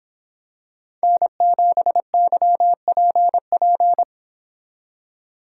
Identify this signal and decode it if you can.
Morse code, heard as a keyed tone, about 26 wpm, 700 Hz. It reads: N7YPP